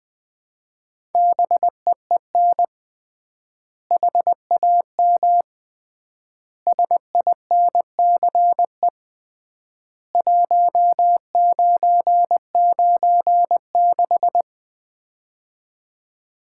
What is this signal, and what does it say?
Morse code, heard as a keyed tone, about 20 wpm, 700 Hz. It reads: BEEN HAM SINCE 1996
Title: Common QSO Phrases - 3x Word-Spacing - 20wpm